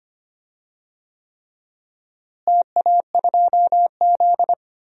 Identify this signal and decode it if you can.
Morse code, heard as a keyed tone, about 25 wpm, 700 Hz. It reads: TA2Z
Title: DXCC-100 Call Signs - 4 Characters - Repeated 3x - 25wpm